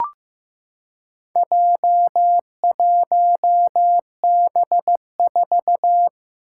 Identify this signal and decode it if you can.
Morse code, heard as a keyed tone, about 15 wpm, 700 Hz. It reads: J1B4
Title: Four Challenging Letter-Number Combinations 15wpm